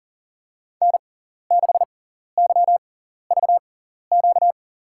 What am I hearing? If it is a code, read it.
Morse code, heard as a keyed tone, about 40 wpm, spaced as for 14 wpm, 700 Hz. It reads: N6YVQ
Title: USA Call Signs - 5 Characters - 6x Character Spacing - 40wpm